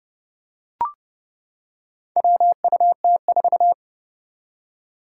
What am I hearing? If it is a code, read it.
Morse code, heard as a keyed tone, about 31 wpm, 700 Hz. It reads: WUT4
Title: Four Challenging Letter-Number Combinations 30wpm